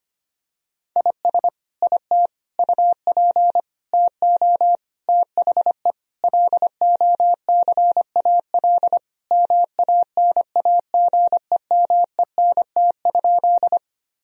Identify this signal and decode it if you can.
Morse code, heard as a keyed tone, about 25 wpm, 700 Hz. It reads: IS IT UP TO THE LOCAL MANAGEMENT?
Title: Sentences from Top 1000 Words - Review Part 4 - 25wpm